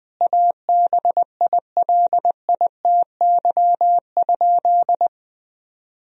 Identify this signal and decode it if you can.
Morse code, heard as a keyed tone, about 20 wpm, 700 Hz. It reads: ABILITY?
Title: Sentences from Top 1000 Words - Review Part 4 - 20wpm